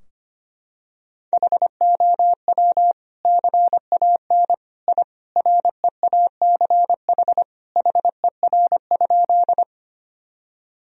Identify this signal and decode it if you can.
Morse code, heard as a keyed tone, about 25 wpm, 700 Hz. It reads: HOW CAN I REACH HER?
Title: Sentences from Top 1000 Words - Review Part 2 - 25wpm